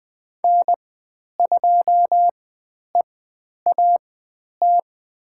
Morse code, heard as a keyed tone, about 20 wpm, spaced as for 10 wpm, 700 Hz. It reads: N2EAT
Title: 102 Lesson 17g - Letter u - Call Signs 20wpm